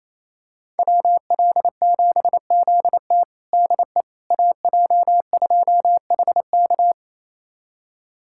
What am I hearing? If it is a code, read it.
Morse code, heard as a keyed tone, about 28 wpm, 700 Hz. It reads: WL7ZT DE AJ2HK
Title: USA Callsign DE Callsign - 5 Characters - 28wpm